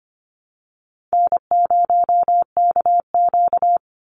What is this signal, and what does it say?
Morse code, heard as a keyed tone, about 25 wpm, 700 Hz. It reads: N0KQ
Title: Easy Call Signs - 25wpm